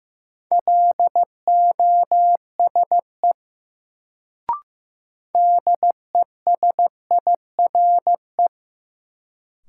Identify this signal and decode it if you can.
Morse code, heard as a keyed tone, about 16 wpm, 700 Hz. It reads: LOSE DESIRE
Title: Sentences from Top 1100 Words - New Word Drill Alt - 15wpm